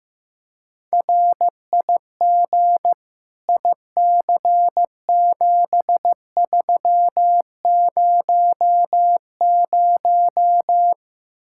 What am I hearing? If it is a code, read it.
Morse code, heard as a keyed tone, about 15 wpm, 700 Hz. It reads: RIG IC7300
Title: Practice QSO - Exchange 2 - Info - 15wpm